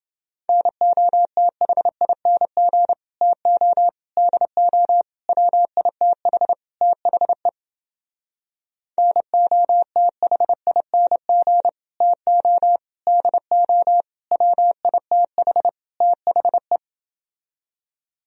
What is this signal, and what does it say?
Morse code, heard as a keyed tone, about 30 wpm, 700 Hz. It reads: NOTHING TO DO WITH THE NOTHING TO DO WITH THE